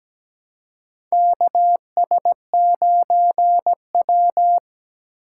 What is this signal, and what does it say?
Morse code, heard as a keyed tone, about 17 wpm, 700 Hz. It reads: KS9W
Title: K1USN SST QSO Practice - 17wpm